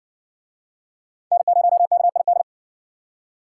Morse code, heard as a keyed tone, about 54 wpm, 700 Hz. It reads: N0KID